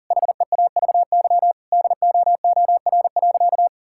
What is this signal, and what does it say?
Morse code, heard as a keyed tone, about 40 wpm, 700 Hz. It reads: HEAVY DOOR.
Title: Sentences from Top 1600 Words 40wpm